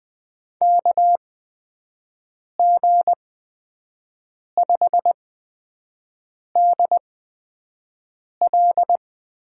Morse code, heard as a keyed tone, about 20 wpm, spaced as for 5 wpm, 700 Hz. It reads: KG5DL
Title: USA Call Signs - 8x Character Spacing - 20wpm